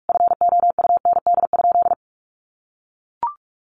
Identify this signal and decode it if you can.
Morse code, heard as a keyed tone, about 44 wpm, 700 Hz. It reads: FOUND?
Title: Sentences from Top 1400 Words 45wpm